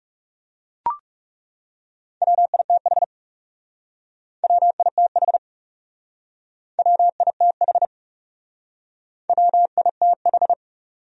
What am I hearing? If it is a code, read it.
Morse code, heard as a keyed tone, about 40 wpm, 700 Hz. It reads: WITH WITH WITH WITH